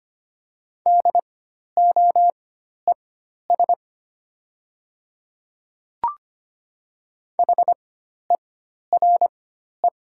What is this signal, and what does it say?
Morse code, heard as a keyed tone, about 25 wpm, spaced as for 12 wpm, 700 Hz. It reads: DOES HERE
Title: 4x Character Spacing - Top 500 Words - 25wpm